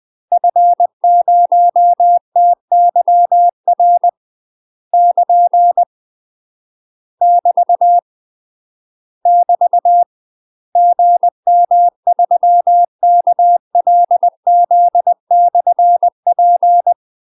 Morse code, heard as a keyed tone, about 20 wpm, 700 Hz. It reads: F0TYR <KN> <BT> <BT> GM3KLZ/P
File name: UK MASTER QSOs TRAINING / 20WPM / UK MASTER QSOs 0044